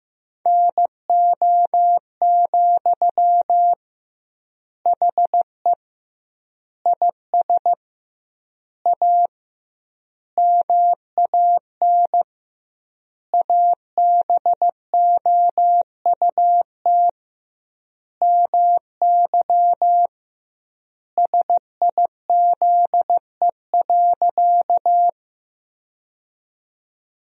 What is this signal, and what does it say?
Morse code, heard as a keyed tone, about 15 wpm, 700 Hz. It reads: NO, HE IS A MAN ABOUT MY SIZE.
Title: 2x Word-Spacing - Sentences from Top 400 Words - 15wpm